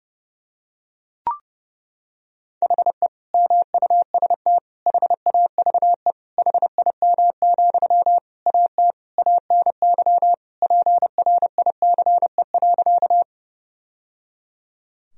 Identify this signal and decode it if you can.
Morse code, heard as a keyed tone, about 30 wpm, 700 Hz. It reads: HE MUST HAVE HIM, AT ANY PRICE.
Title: Sentences from Top 1000 Words - Review Part 4 - 30wpm